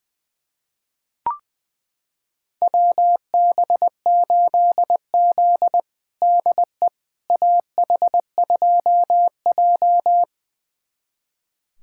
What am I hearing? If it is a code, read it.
Morse code, heard as a keyed tone, about 20 wpm, 700 Hz. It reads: WB8Z DE AH2J
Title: USA Callsign DE Callsign - 4 Characters - 20wpm